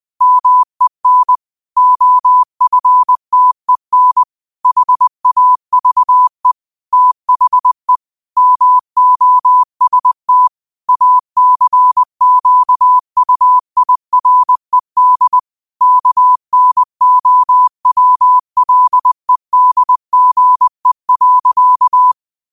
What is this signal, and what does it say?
Morse code, heard as a keyed tone, about 20 wpm, 1000 Hz. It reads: MEN OFTEN HAVE THE MOST ACQUIRED KNOWLEDGE.
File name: qotdfiles / 20wpm